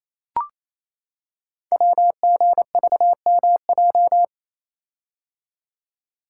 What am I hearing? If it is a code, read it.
Morse code, heard as a keyed tone, about 27 wpm, 700 Hz. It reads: WGVMJ
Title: Code Groups - 5 Letters - 28wpm